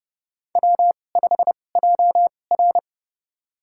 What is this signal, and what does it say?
Morse code, heard as a keyed tone, about 30 wpm, spaced as for 22 wpm, 700 Hz. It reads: W5JR
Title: USA Call Signs - 4 Characters - 2x Character Spacing - 30wpm